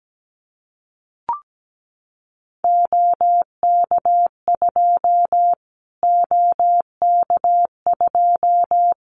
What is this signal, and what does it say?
Morse code, heard as a keyed tone, about 17 wpm, 700 Hz. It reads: OK2 OK2
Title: Callsign Prefixes for Top 25 DXCC - Repeated 3x - 17wpm